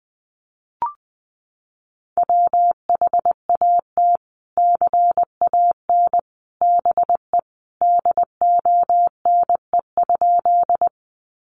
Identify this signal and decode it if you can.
Morse code, heard as a keyed tone, about 20 wpm, 700 Hz. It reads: WHAT CAN BE DONE?
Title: Sentences from Top 500 Words 20wpm